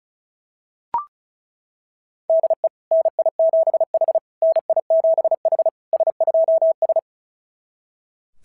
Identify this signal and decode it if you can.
Morse code, heard as a keyed tone, about 36 wpm, 660 Hz. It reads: DE NI7H NI7H S2S
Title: SOTA QSO Practice 35wpm